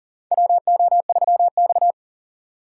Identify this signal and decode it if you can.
Morse code, heard as a keyed tone, about 40 wpm, 700 Hz. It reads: WO3X
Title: USA Call Signs - 4 Characters - 3x Character Spacing - 40wpm